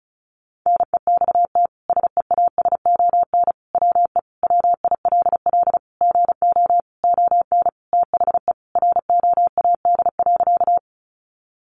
Morse code, heard as a keyed tone, about 34 wpm, 700 Hz. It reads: NEXT SEASON WE WILL GO ON THE ROAD.
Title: Sentences from Top 1400 Words 35wpm